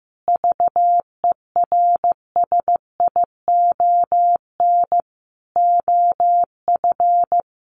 Morse code, heard as a keyed tone, about 15 wpm, 700 Hz. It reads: VERSION OF
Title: Sentences from Top 1400 Words 15wpm